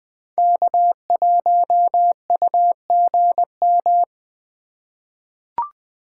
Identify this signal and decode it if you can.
Morse code, heard as a keyed tone, about 20 wpm, 700 Hz. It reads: K1UGM
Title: USA Call Signs - 5 Characters - 8x Character Spacing - 20wpm